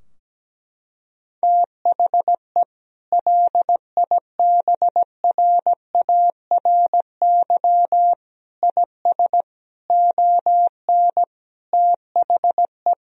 Morse code, heard as a keyed tone, about 17 wpm, 700 Hz. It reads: THE LIBRARY IS ON THE
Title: Sentences from Top 1000 Words - Review Part 4 - 17wpm